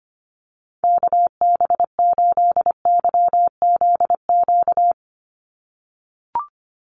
Morse code, heard as a keyed tone, about 25 wpm, 700 Hz. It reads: KB8YZQ
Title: USA Call Signs 25wpm